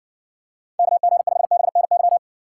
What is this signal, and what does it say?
Morse code, heard as a keyed tone, about 55 wpm, 700 Hz. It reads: KG5BNY